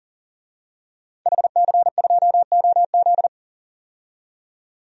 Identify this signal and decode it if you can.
Morse code, heard as a keyed tone, about 40 wpm, 700 Hz. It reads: HC2OZ
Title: DXCC-100 Call Signs - 5 Characters - 40wpm